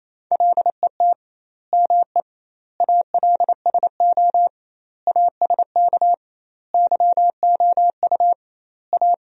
Morse code, heard as a keyed tone, about 28 wpm, 700 Hz. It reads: LET ME ALSO ASK YOU A
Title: 2x Word-Spacing - Sentences from Top 400 Words - 28wpm